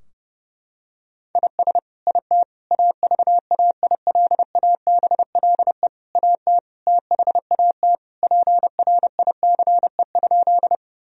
Morse code, heard as a keyed tone, about 32 wpm, 700 Hz. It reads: IS IT AVAILABLE AT THAT PRICE?
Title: Sentences from Top 1000 Words - Review Part 3 - 30wpm